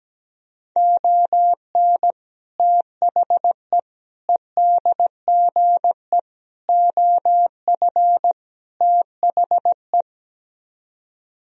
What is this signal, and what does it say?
Morse code, heard as a keyed tone, about 17 wpm, 700 Hz. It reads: ON THE EDGE OF THE
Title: Speed-Racing - Sets of 5 Words 17wpm